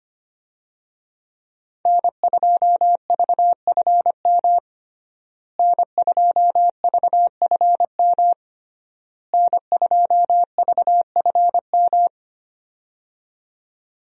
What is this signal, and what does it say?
Morse code, heard as a keyed tone, about 25 wpm, 700 Hz. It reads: N2VFM N2VFM N2VFM
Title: USA Call Signs - 5 Characters - Repeated 3x - 25wpm